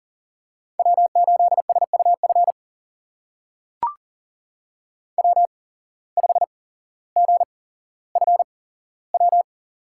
Morse code, heard as a keyed tone, about 40 wpm, spaced as for 17 wpm, 700 Hz. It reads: W8SUF W5GFW
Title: USA Call Signs - 5 Characters - 8x Character Spacing - 40wpm